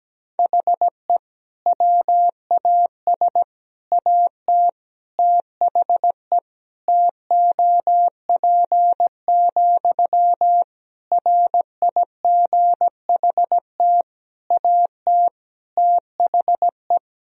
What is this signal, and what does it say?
Morse code, heard as a keyed tone, about 17 wpm, 700 Hz. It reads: HE WAS AT THE TOP, RIGHT AT THE
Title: Sentences from Top 1000 Words - Review Part 2 - 17wpm